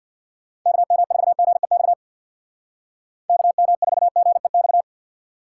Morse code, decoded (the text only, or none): KM4GEX KM4GEX